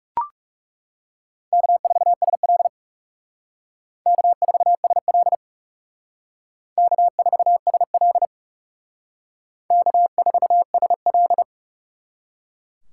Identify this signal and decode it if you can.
Morse code, heard as a keyed tone, about 39 wpm, 700 Hz. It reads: K4SL K4SL K4SL K4SL